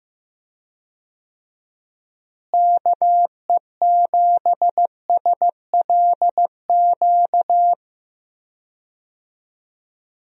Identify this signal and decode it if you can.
Morse code, heard as a keyed tone, about 15 wpm, 700 Hz. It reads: KE7SLQ
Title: USA Call Signs - 6 Characters - Repeated 3x - 15wpm